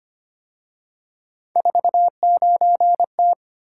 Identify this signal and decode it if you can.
Morse code, heard as a keyed tone, about 29 wpm, 700 Hz. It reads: 49T
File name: Speed-Racing - Three Letter-Number Combinations - 25wpm